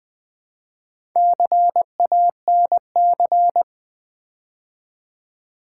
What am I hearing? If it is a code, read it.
Morse code, heard as a keyed tone, about 20 wpm, 700 Hz. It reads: CANC
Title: Build A Word - 2 Letters At A Time - 6-Letter Words - 20wpm